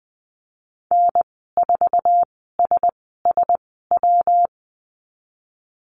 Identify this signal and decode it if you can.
Morse code, heard as a keyed tone, about 20 wpm, spaced as for 15 wpm, 700 Hz. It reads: N4SSW
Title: 167 Lesson 27g - Number 7 - Call Signs 20wpm